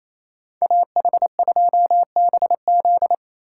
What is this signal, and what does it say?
Morse code, heard as a keyed tone, about 37 wpm, 700 Hz. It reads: AH2BZ